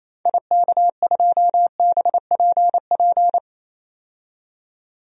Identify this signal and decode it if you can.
Morse code, heard as a keyed tone, about 28 wpm, 700 Hz. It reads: IK2BPP
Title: DXCC-100 Call Signs - 6 Characters - 28wpm